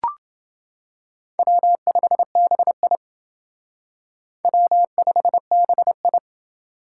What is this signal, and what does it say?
Morse code, heard as a keyed tone, about 28 wpm, 700 Hz. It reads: W5BI W5BI